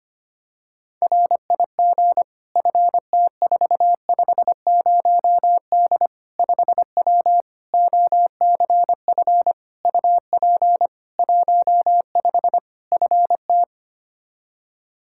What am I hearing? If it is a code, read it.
Morse code, heard as a keyed tone, about 25 wpm, 700 Hz. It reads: RIG FT450D 5W OCF UP 15 FT
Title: K1USN SST QSO Practice 25wpm